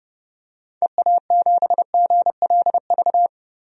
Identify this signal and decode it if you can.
Morse code, heard as a keyed tone, about 30 wpm, 700 Hz. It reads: EA7GLV
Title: DXCC-100 Call Signs - 6 Characters - 30wpm